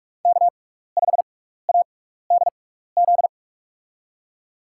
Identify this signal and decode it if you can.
Morse code, heard as a keyed tone, about 44 wpm, spaced as for 15 wpm, 700 Hz. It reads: K5ADZ
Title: USA Call Signs - 5 Characters - 6x Character Spacing - 45wpm